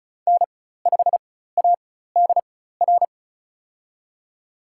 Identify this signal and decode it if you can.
Morse code, heard as a keyed tone, about 36 wpm, spaced as for 16 wpm, 700 Hz. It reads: N5ADR
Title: USA Call Signs - 5 Characters - 4x Character Spacing - 35wpm